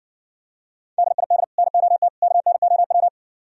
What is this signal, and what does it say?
Morse code, heard as a keyed tone, about 58 wpm, 700 Hz. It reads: DID NOT KNOW